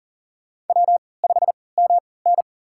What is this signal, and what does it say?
Morse code, heard as a keyed tone, about 40 wpm, spaced as for 23 wpm, 700 Hz. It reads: W5MN
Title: USA Call Signs - 4 Characters - 3x Character Spacing - 40wpm